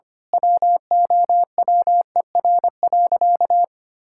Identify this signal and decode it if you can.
Morse code, heard as a keyed tone, about 25 wpm, 700 Hz. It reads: WOWER.
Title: Pangram Challenge v2 - 25wpm